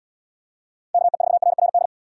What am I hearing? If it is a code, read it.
Morse code, heard as a keyed tone, about 79 wpm, 700 Hz. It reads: KE4RRD